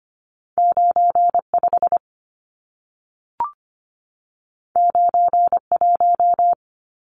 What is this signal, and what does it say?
Morse code, heard as a keyed tone, about 25 wpm, 700 Hz. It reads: 95 91
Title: Numbers - 2-Digit 25wpm